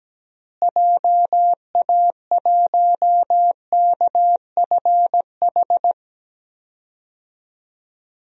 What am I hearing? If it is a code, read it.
Morse code, heard as a keyed tone, about 17 wpm, 700 Hz. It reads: JA1KFH
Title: DXCC-100 Call Signs - 6 Characters - Repeated 3x - 17wpm